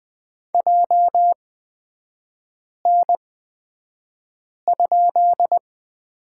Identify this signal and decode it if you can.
Morse code, heard as a keyed tone, about 20 wpm, spaced as for 5 wpm, 700 Hz. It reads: JN?
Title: 250 Lesson 40e - BK - Three Characters 20wpm